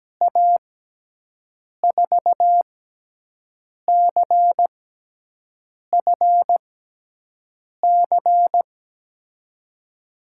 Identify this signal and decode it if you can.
Morse code, heard as a keyed tone, about 17 wpm, spaced as for 6 wpm, 700 Hz. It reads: A4CFC
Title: USA Call Signs - 6 Characters - 6x Character Spacing - 17wpm